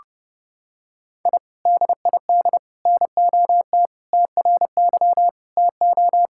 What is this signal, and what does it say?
Morse code, heard as a keyed tone, about 30 wpm, 700 Hz. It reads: I DID NOT TRY TO
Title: Sentences from Top 1000 Words - Review Part 5 - 30wpm